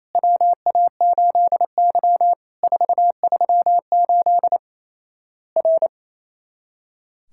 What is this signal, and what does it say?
Morse code, heard as a keyed tone, about 28 wpm, 700 Hz. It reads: WA8Y 438 R
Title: Worked All Europe DX Contest - QTC Exchange - 28wpm